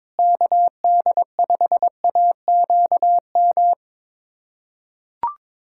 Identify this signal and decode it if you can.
Morse code, heard as a keyed tone, about 22 wpm, 700 Hz. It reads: KD5AQM
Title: USA Callsign DE Callsign 22wpm